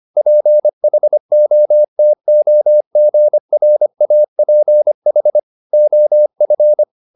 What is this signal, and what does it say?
Morse code, heard as a keyed tone, about 25 wpm, 600 Hz. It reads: PHOTOGRAPH OF
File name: the old man and the sea part 1 25wpm